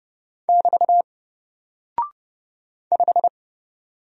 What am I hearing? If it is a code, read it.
Morse code, heard as a keyed tone, about 30 wpm, 700 Hz. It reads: <BT> 5
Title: Single Character (Letter:Number:Punctuation:Prosign) Alt 30wpm